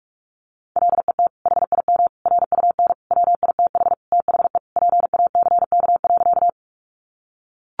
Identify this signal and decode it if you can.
Morse code, heard as a keyed tone, about 44 wpm, 700 Hz. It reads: LET HIM RUN WITH THE PACK.
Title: Sentences from Top 1300 Words 45wpm